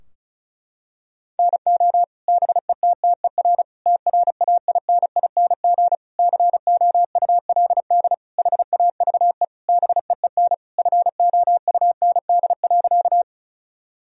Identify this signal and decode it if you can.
Morse code, heard as a keyed tone, about 36 wpm, 700 Hz. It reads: NO BETTER TRAINING COULD HAVE BEEN FOUND.
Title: Sentences from Top 1000 Words - Review Part 4 - 35wpm